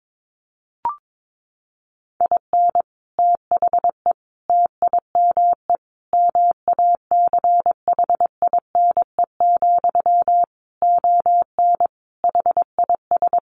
Morse code, heard as a keyed tone, about 22 wpm, 700 Hz. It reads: IN THE TIME MACHINE, ON HIS
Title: A Sound Of Thunder 22wpm